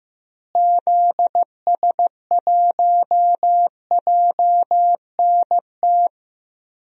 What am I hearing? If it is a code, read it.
Morse code, heard as a keyed tone, about 16 wpm, 700 Hz. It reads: ZS1JNT
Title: DXCC-100 Call Signs - 6 Characters - Repeated 3x - 15wpm